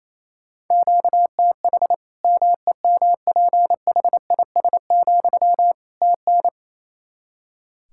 Morse code, heard as a keyed tone, about 28 wpm, 700 Hz. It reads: QTH MEMPHIS, TN
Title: Practice QSO - Exchange 1 - Beginning - 28wpm